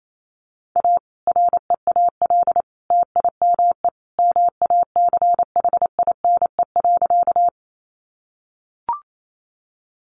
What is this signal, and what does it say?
Morse code, heard as a keyed tone, about 28 wpm, 700 Hz. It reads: A REAL TIME MACHINE.
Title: A Sound Of Thunder 28wpm